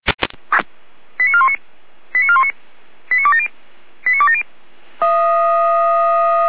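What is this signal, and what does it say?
Five-tone selective call (ZVEI), heard as quick runs of five tones, tones of about 80 ms. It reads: 89419 89419 89270 89270
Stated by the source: ZVEI five-tone selective call received by SDR, Feuerwehr Orlamünde.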